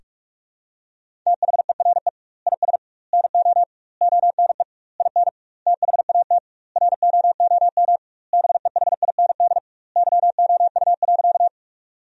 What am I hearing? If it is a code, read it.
Morse code, heard as a keyed tone, about 45 wpm, 700 Hz. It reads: THERE IS NO ONE IN THAT ROOM BEHIND YOU.
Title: 2x Word-Spacing - Sentences from Top 500 Words - 45wpm